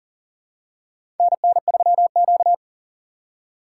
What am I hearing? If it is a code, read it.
Morse code, heard as a keyed tone, about 42 wpm, 690 Hz. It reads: NN3Q
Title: USA Call Signs - 4x Character Spacing - 40wpm